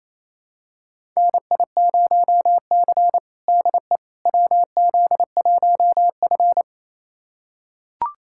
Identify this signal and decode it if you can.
Morse code, heard as a keyed tone, about 28 wpm, 700 Hz. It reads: NI0C DE WZ1F
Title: USA Callsign DE Callsign - 4 Characters - 28wpm